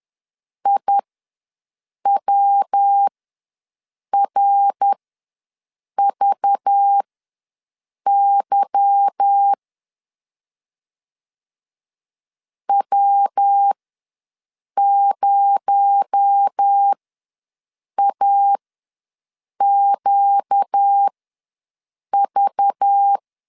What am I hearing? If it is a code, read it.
Morse code, heard as a keyed tone, about 10 wpm, spaced as for 6 wpm, 800 Hz. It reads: IWRVY W0AQV
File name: LEZIONI / 8 YJ45 / 8 YJ ws 10wpm 4043